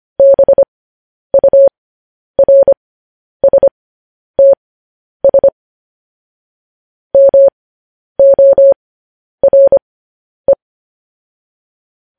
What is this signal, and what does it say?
Morse code, heard as a keyed tone, about 25 wpm, spaced as for 10 wpm, 550 Hz. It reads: BURSTS MORE